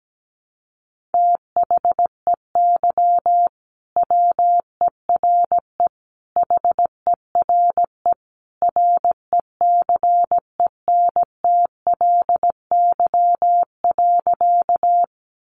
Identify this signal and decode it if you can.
Morse code, heard as a keyed tone, about 17 wpm, 700 Hz. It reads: THEY WERE HERE RECENTLY.
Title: Sentences from Top 1200 Words 17wpm